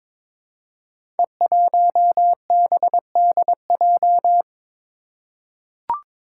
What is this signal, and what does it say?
Morse code, heard as a keyed tone, about 22 wpm, 700 Hz. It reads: E1BDJ
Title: Code Groups - 5 Letter:Numbers - 22wpm